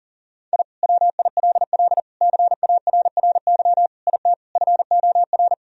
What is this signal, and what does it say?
Morse code, heard as a keyed tone, about 40 wpm, 700 Hz. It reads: I WILL CARRY IT FOR
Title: Sentences from Top 1000 Words - Review Part 1 - 40wpm